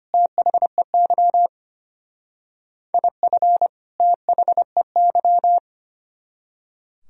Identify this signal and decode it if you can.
Morse code, heard as a keyed tone, about 32 wpm, 700 Hz. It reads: THEY IF THEY